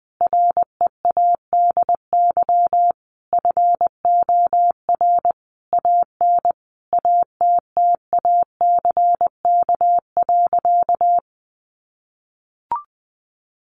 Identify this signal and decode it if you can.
Morse code, heard as a keyed tone, about 20 wpm, 700 Hz. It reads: READY FOR AN ATTACK.
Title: Sentences from Top 1500 Words 20wpm